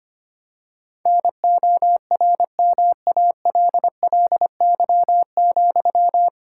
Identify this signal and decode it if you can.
Morse code, heard as a keyed tone, about 25 wpm, 700 Hz. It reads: NORMALLY,
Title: Sentences from Top 1000 Words - Review Part 6 - 25wpm